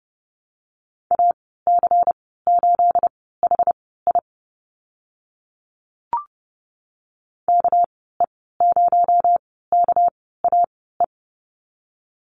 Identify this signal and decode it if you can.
Morse code, heard as a keyed tone, about 30 wpm, spaced as for 17 wpm, 700 Hz. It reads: AC8HI KE0KAE
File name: Lesson 37g - Letter x - Call Signs 30wpm